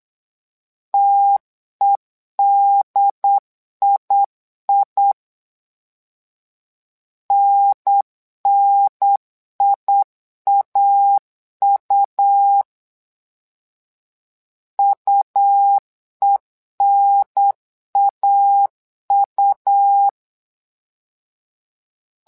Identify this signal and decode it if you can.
Morse code, heard as a keyed tone, about 8 wpm, 800 Hz. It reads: TEDII NNIAU UENAU
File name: LEZIONI / 2 UHDN / 2 UHDN 8wpm2232